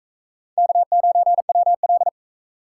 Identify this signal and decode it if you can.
Morse code, heard as a keyed tone, about 42 wpm, 700 Hz. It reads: K9WL